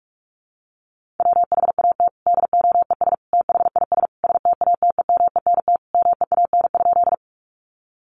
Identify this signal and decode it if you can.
Morse code, heard as a keyed tone, about 44 wpm, 700 Hz. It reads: WHAT DOES THIS STATEMENT MEAN?
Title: Sentences from Top 1200 Words 45wpm